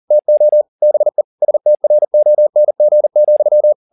Morse code, heard as a keyed tone, about 40 wpm, 600 Hz. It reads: TO BE STRONG,
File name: the old man and the sea 2 40wpm